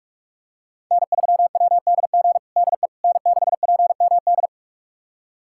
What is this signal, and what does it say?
Morse code, heard as a keyed tone, about 46 wpm, 700 Hz. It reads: N3WDG DE N6PMD
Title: USA Callsign DE Callsign - 5 Characters - 45wpm